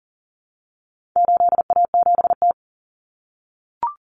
Morse code, heard as a keyed tone, about 40 wpm, 700 Hz. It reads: 8A7T
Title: Common Cut Numbers - 4-Digit 40wpm